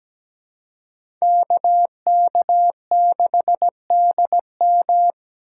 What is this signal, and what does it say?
Morse code, heard as a keyed tone, about 17 wpm, 700 Hz. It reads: KK6DM
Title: USA Call Signs - 5 Characters - 2x Character Spacing - 17wpm